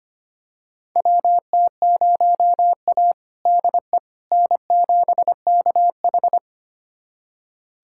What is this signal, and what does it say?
Morse code, heard as a keyed tone, about 25 wpm, 700 Hz. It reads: WT0A DE N7KH
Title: USA Callsign DE Callsign - 4 Characters - 25wpm